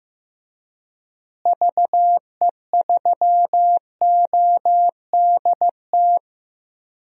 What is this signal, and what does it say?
Morse code, heard as a keyed tone, about 15 wpm, 700 Hz. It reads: VE3ODT
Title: DXCC-100 Call Signs - 6 Characters - Repeated 3x - 15wpm